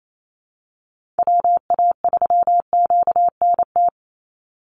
Morse code, heard as a keyed tone, about 28 wpm, 700 Hz. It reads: WA3QNT
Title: Call Signs Repeated 3x - 28wpm